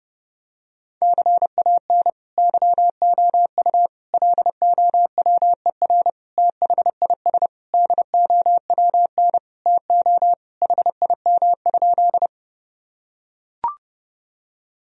Morse code, heard as a keyed tone, about 30 wpm, 700 Hz. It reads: CAN YOU LOWER THIS DOWN TO HIM?
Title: Sentences from Top 1000 Words - Review Part 4 - 30wpm